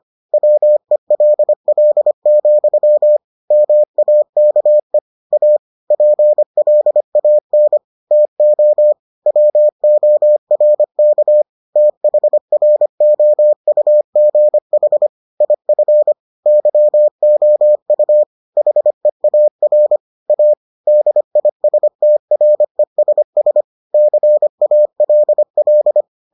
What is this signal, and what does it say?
Morse code, heard as a keyed tone, about 25 wpm, 600 Hz. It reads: WELL, MAKE A PLAN TO WORK THROUGH IF YOU HEAR A DISTRESS CALL